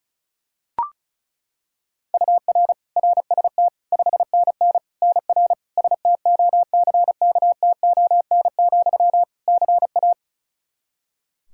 Morse code, heard as a keyed tone, about 34 wpm, 700 Hz. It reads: UR RST 5NN NR STOCKTON, CA
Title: Practice QSO - Exchange 1 - Beginning - 35wpm